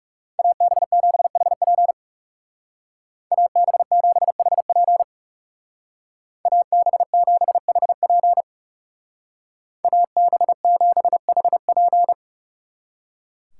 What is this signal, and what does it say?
Morse code, heard as a keyed tone, about 34 wpm, 700 Hz. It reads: AB7HP AB7HP AB7HP AB7HP
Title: Speed Racing - USA 2x2 Call Signs 30wpm